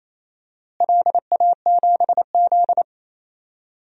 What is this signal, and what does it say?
Morse code, heard as a keyed tone, about 28 wpm, 700 Hz. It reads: LA7Z
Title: DXCC-100 Call Signs - 4 Characters - Repeated 3x - 28wpm